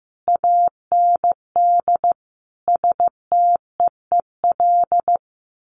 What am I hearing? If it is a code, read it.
Morse code, heard as a keyed tone, about 15 wpm, 700 Hz. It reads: AND STEEL
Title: A Sound Of Thunder 15wpm